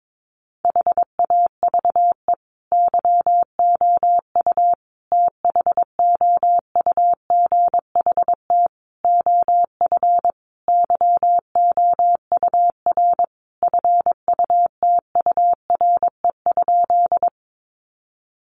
Morse code, heard as a keyed tone, about 22 wpm, 700 Hz. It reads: HAVE YOU THOUGHT OF YOUR FUTURE?
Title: Sentences from Top 700 Words 22wpm